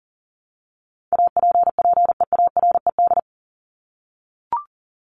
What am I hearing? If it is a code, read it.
Morse code, heard as a keyed tone, about 40 wpm, 700 Hz. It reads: APPEARED
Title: ED Suffix - Top 100 Words with Base Words 40wpm